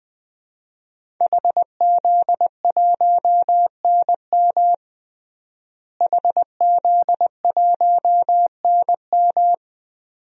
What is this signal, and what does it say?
Morse code, heard as a keyed tone, about 20 wpm, 700 Hz. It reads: HZ1NM HZ1NM
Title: DXCC-100 Call Signs - 5 Characters - Repeated 2x - 20wpm